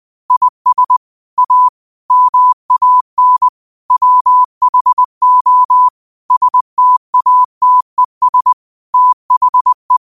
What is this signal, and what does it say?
Morse code, heard as a keyed tone, about 20 wpm, 1000 Hz. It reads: IS A MAN WHO STATES THE
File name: qotdfiles / 20wpm